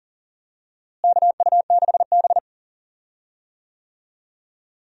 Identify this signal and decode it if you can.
Morse code, heard as a keyed tone, about 39 wpm, 700 Hz. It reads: KU6B
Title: USA Call Signs - 4 Characters - Repeated 3x - 40wpm